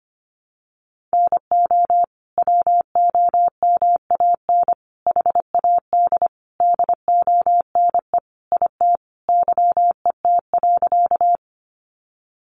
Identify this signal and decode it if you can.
Morse code, heard as a keyed tone, about 25 wpm, 700 Hz. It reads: NO WOMAN HAD DONE IT YET.
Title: Sentences from Top 1100 Words 25wpm